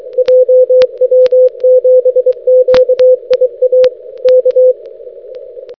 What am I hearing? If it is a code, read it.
Morse code, heard as a keyed tone, about 24 wpm, 510 Hz. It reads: JW7QIA K
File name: jw7qia-6m